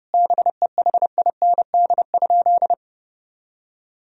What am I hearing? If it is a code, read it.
Morse code, heard as a keyed tone, about 30 wpm, 700 Hz. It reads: BEHIND?
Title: Sentences from Top 1000 Words - Review Part 2 - 30wpm